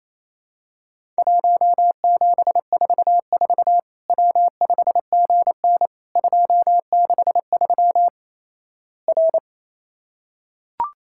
Morse code, heard as a keyed tone, about 28 wpm, 700 Hz. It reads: 1744 W5GN 263 R
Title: Worked All Europe DX Contest - QTC Exchange - 28wpm